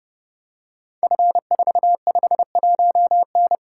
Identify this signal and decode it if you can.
Morse code, heard as a keyed tone, about 30 wpm, 700 Hz. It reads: F451N
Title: Code Groups - 5 Letter:Numbers - 30wpm